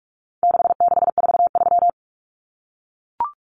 Morse code, heard as a keyed tone, about 44 wpm, 700 Hz. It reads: 6643
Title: Common Cut Numbers - 4-Digit 45wpm